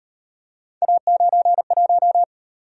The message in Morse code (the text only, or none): A91